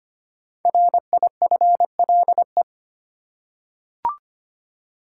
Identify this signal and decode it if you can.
Morse code, heard as a keyed tone, about 25 wpm, 700 Hz. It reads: RIFLE
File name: 5 Letter Words - First 3 Letters Given - 25wpm